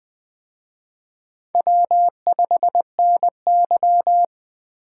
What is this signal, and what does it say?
Morse code, heard as a keyed tone, about 20 wpm, 700 Hz. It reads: W5NY
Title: Speed Racing - USA Call Signs 17wpm